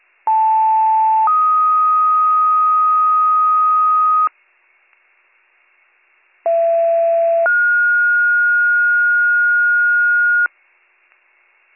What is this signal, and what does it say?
Two-tone page, heard as a one-second tone, then a three-second tone, about 880 then 1250 Hz, about 670 then 1430 Hz. It is heard from 0.3 to 10.5 s.